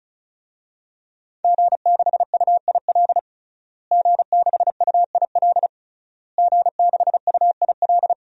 Morse code, heard as a keyed tone, about 36 wpm, 700 Hz. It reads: G6UIL G6UIL G6UIL
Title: DXCC-100 Call Signs - 5 Characters - Repeated 3x - 35wpm